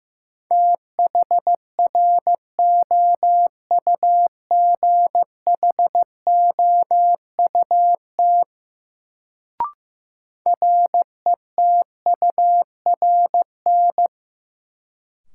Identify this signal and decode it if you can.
Morse code, heard as a keyed tone, about 15 wpm, 700 Hz. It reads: THROUGHOUT RETURN
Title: Sentences from Top 800 Words - New Word Drill - 15wpm